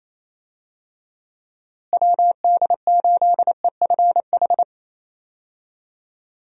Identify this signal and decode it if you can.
Morse code, heard as a keyed tone, about 27 wpm, 700 Hz. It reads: WD8EFH
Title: USA Call Signs - 6 Characters - Repeated 3x - 28wpm